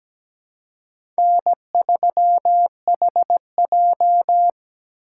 Morse code, heard as a keyed tone, about 17 wpm, 700 Hz. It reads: N3HJ
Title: USA Call Signs - 4 Characters - 4x Character Spacing - 17wpm